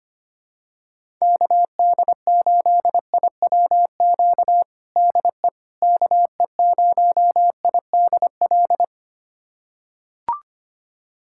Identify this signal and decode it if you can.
Morse code, heard as a keyed tone, about 25 wpm, 700 Hz. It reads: KD8IWQ DE KE0IDL
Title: USA Callsign DE Callsign - 6 Characters - 25wpm